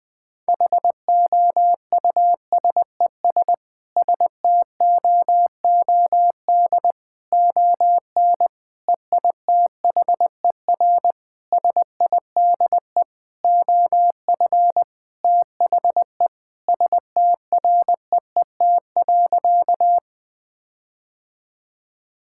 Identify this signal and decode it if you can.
Morse code, heard as a keyed tone, about 19 wpm, 700 Hz. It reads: HOUSES STOOD ON EITHER SIDE OF THE STREET.
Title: Sentences with 5-Word Phrases - 20wpm